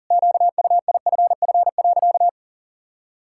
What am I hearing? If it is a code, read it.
Morse code, heard as a keyed tone, about 40 wpm, 700 Hz. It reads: QUIFF.
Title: Pangram Challenge v2 - 40wpm